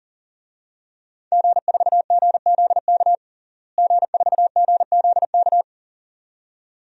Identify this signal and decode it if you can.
Morse code, heard as a keyed tone, about 40 wpm, 700 Hz. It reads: G4GZK G4GZK
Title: DXCC-100 Call Signs - 5 Characters - Repeated 2x - 40wpm